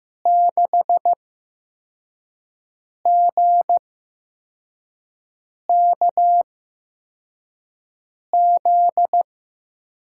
Morse code, heard as a keyed tone, about 15 wpm, spaced as for 4 wpm, 700 Hz. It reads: 6GKZ
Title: USA Call Signs - 6 Characters - 8x Character Spacing - 15wpm